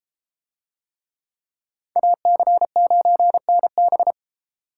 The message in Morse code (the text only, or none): AC9NB